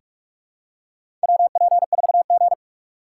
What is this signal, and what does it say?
Morse code, heard as a keyed tone, about 46 wpm, 700 Hz. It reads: WP4G
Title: USA Call Signs - 4 Characters - 3x Character Spacing - 45wpm